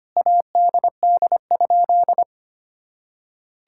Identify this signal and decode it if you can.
Morse code, heard as a keyed tone, about 25 wpm, 700 Hz. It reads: ADD?
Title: Sentences from Top 1000 Words - Review Part 5 - 25wpm